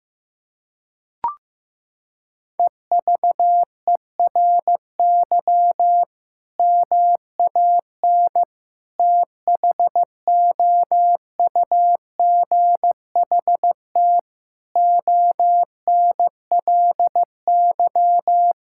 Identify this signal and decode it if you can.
Morse code, heard as a keyed tone, about 15 wpm, 700 Hz. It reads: EVERY MAN THOUGHT ONLY
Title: Sentences from Top 1600 Words 15wpm